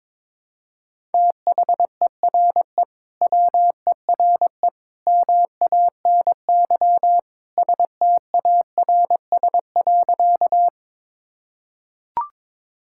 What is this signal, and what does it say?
Morse code, heard as a keyed tone, about 22 wpm, 700 Hz. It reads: THERE WERE MANY STARS.
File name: practice / 22wpm / Sentences with 2-Word Phrases - 22wpm